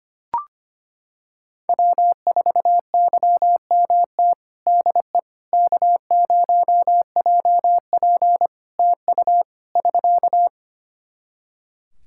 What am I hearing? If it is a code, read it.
Morse code, heard as a keyed tone, about 25 wpm, 700 Hz. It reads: W4YMT DE K0JP TU <SK>
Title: Practice QSO - Exchange 4 - Ending - 25wpm